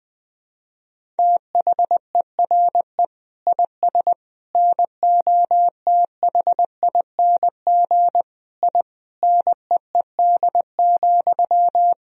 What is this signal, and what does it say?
Morse code, heard as a keyed tone, about 19 wpm, 700 Hz. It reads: THERE IS NOTHING I NEED,
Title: Sentences from Top 1000 Words - Review Part 1 - 20wpm